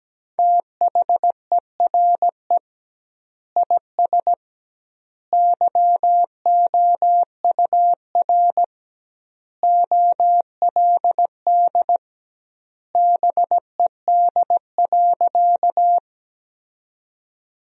Morse code, heard as a keyed tone, about 17 wpm, 700 Hz. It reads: THERE IS YOUR OLD BED.
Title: 2x Word-Spacing - Sentences from Top 500 Words - 17wpm